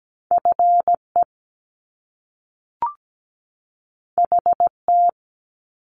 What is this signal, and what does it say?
Morse code, heard as a keyed tone, about 17 wpm, 700 Hz. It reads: FE HT
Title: Top Two Letter Combinations in Words 17wpm